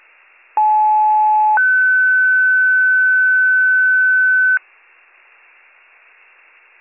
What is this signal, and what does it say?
Two-tone page, heard as a one-second tone, then a three-second tone, about 860 then 1500 Hz. It is heard from 0.6 to 4.6 s.